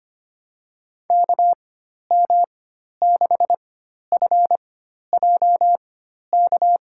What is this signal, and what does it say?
Morse code, heard as a keyed tone, about 25 wpm, spaced as for 12 wpm, 700 Hz. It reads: KM6FJK
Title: USA Call Signs - 4x Character Spacing - 25wpm